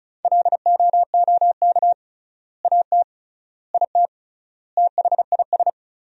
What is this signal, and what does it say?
Morse code, heard as a keyed tone, about 36 wpm, 700 Hz. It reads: LOOK AT IT THIS
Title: 3x Word-Spacing - Sentences from Top 100 Words - 35wpm